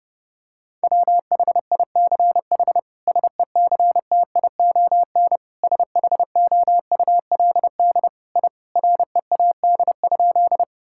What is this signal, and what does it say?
Morse code, heard as a keyed tone, about 30 wpm, 700 Hz. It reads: WHICH SECTION SHOULD I READ?
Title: Sentences from Top 1000 Words - Review Part 5 - 30wpm